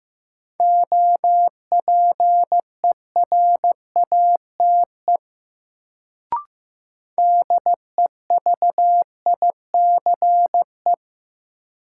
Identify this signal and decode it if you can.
Morse code, heard as a keyed tone, about 15 wpm, 700 Hz. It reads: OPERATE DEVICE
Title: Sentences from Top 1500 Words - New Word Drill - 15wpm